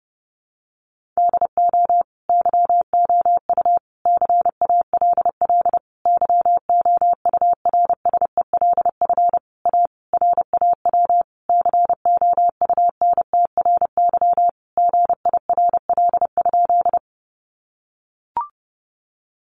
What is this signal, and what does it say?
Morse code, heard as a keyed tone, about 30 wpm, 700 Hz. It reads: DO YOU CALL YOURSELF A RAW COUNTRY GIRL?
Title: Sentences from Top 1500 Words 30wpm